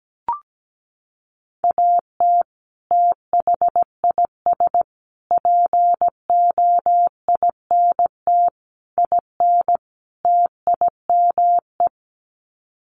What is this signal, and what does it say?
Morse code, heard as a keyed tone, about 17 wpm, 700 Hz. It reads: AT THIS POINT IN TIME
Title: Sets of 5 Words Encore 17wpm